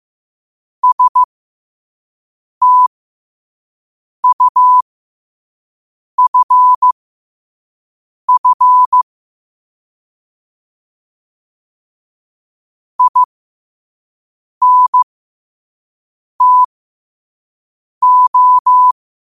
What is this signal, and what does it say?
Morse code, heard as a keyed tone, about 15 wpm, spaced as for 5 wpm, 1000 Hz. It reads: STUFF INTO
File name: qotdfiles / 5wpm